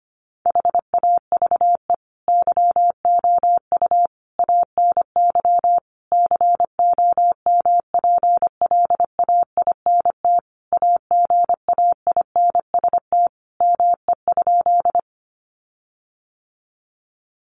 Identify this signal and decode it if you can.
Morse code, heard as a keyed tone, about 25 wpm, 700 Hz. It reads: HAVE YOU ANY COMPLAINT AGAINST ME?
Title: Sentences from Top 1400 Words 25wpm